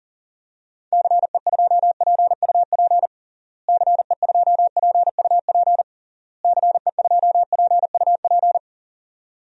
Morse code, heard as a keyed tone, about 40 wpm, 700 Hz. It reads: CE2PUP CE2PUP CE2PUP
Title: DXCC-100 Call Signs - 6 Characters - Repeated 3x - 40wpm